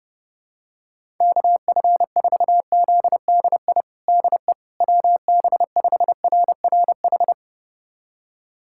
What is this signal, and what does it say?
Morse code, heard as a keyed tone, about 32 wpm, 700 Hz. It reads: KF4ZDI DE WB5RRH
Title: USA Callsign DE Callsign - 6 Characters - 30wpm